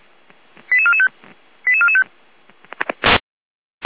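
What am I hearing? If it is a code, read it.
Five-tone selective call (ZVEI), heard as quick runs of five tones, tones of about 70 ms. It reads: 88445 88445
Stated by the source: ZVEI five-tone selective call received on an SDR, from the Feuerwehr Orlamünde fire brigade audio server.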